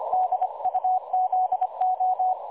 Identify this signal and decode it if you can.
Morse code, heard as a keyed tone, about 25 wpm, 740 Hz. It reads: DUZO
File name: dlazaawansowanych / nagraniazpasma / websdr recording 3565.0kHz